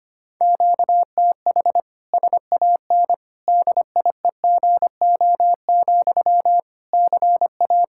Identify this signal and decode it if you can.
Morse code, heard as a keyed tone, about 25 wpm, 700 Hz. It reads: QTH SAN DIEGO, CA
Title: Practice QSO - Exchange 1 - Beginning - 25wpm